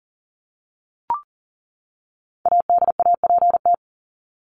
Morse code, heard as a keyed tone, about 40 wpm, 700 Hz. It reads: ADAPT
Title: Top 1501-1750 Words 40wpm